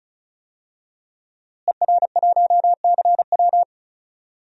Morse code, heard as a keyed tone, about 35 wpm, 700 Hz. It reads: ER1CW
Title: DXCC-100 Call Signs - 5 Characters - Repeated 3x - 35wpm